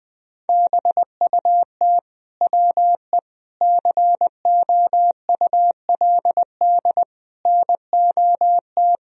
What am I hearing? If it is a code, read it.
Morse code, heard as a keyed tone, about 21 wpm, 700 Hz. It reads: BUT WE COULD NOT
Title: Sentences from Top 1000 Words - Review Part 2 - 20wpm